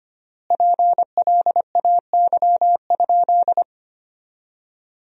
Morse code, heard as a keyed tone, about 25 wpm, 700 Hz. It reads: PLAY?
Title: 3x Word-Spacing - Sentences from Top 200 Words - 25wpm